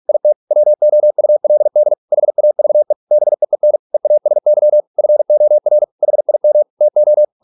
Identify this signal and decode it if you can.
Morse code, heard as a keyed tone, about 46 wpm, 600 Hz. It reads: IT WOULD HAVE BEEN EASY FOR HIM TO